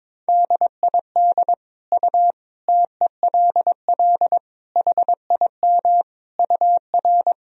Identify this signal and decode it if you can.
Morse code, heard as a keyed tone, about 21 wpm, 700 Hz. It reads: DID U TELL HIM UR
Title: Sentences from Top 400 Words and Abbreviations 22wpm